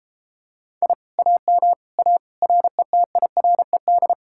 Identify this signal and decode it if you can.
Morse code, heard as a keyed tone, about 33 wpm, 700 Hz. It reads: I AM A RETIRED